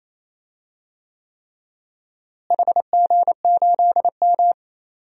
Morse code, heard as a keyed tone, about 28 wpm, 700 Hz. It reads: HG8M
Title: DXCC-100 Call Signs - 4 Characters - Repeated 3x - 28wpm